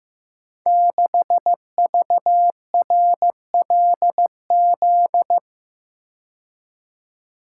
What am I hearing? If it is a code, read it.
Morse code, heard as a keyed tone, about 15 wpm, 700 Hz. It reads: 6VRLZ
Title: Code Groups - 5 Letter:Numbers - 15wpm